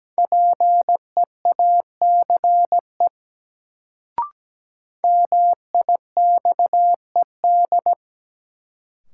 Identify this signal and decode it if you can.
Morse code, heard as a keyed tone, about 17 wpm, 700 Hz. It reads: PEACE MIXED
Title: Sentences from Top 1500 Words - New Word Drill Alt - 17wpm